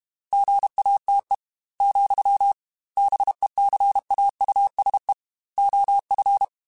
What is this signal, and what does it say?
Morse code, heard as a keyed tone, about 32 wpm, 800 Hz. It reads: GATE, BECAUSE OF
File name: Technical Subjects / Technical Subjects-mp3s / OR Gate @30